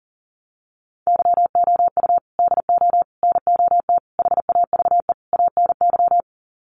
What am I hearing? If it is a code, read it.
Morse code, heard as a keyed tone, about 40 wpm, 700 Hz. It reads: YOU DO NOT HAVE ANY
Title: Sets of 5 Words Encore 40wpm